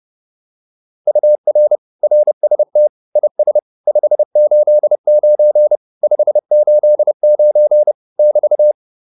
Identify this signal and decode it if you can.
Morse code, heard as a keyed tone, about 30 wpm, 600 Hz. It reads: UR RST IS 589 589 <BT>
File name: QSO302 30